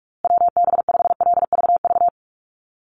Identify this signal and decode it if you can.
Morse code, heard as a keyed tone, about 44 wpm, 700 Hz. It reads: WB5LVV
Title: USA Call Signs Repeated 2x 45wpm